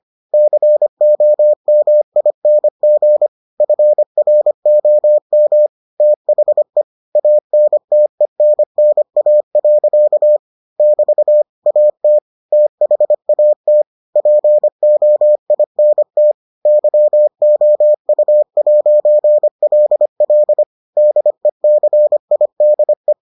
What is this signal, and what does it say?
Morse code, heard as a keyed tone, about 25 wpm, 600 Hz. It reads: COMING FROM THE ANTENNA. <BT> AT THAT POINT YOU'LL DECIDE